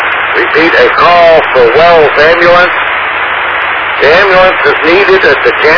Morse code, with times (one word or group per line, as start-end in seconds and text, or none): none